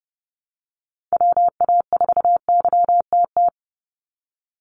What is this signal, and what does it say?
Morse code, heard as a keyed tone, about 30 wpm, 700 Hz. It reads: WA4YTT
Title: USA Call Signs 30wpm